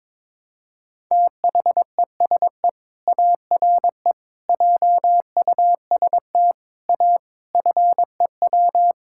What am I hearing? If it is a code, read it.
Morse code, heard as a keyed tone, about 22 wpm, 700 Hz. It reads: THESE ARE JUST A FEW
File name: Sentences with 7-Word Phrases - 22wpm